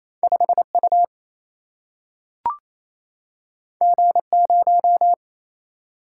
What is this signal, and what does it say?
Morse code, heard as a keyed tone, about 28 wpm, 700 Hz. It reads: 5 U G 0
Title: Single-Letter Single-Number 28wpm